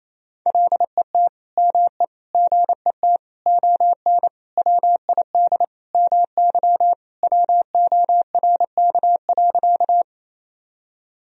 Morse code, heard as a keyed tone, about 28 wpm, 700 Hz. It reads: LET ME GET ON WID MY WORK.
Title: Sentences from Top 300 Words and Abbreviations 28wpm